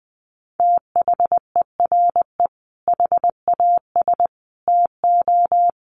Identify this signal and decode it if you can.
Morse code, heard as a keyed tone, about 19 wpm, 700 Hz. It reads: THERE HAS TO
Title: Sets of 5 Words Encore 20wpm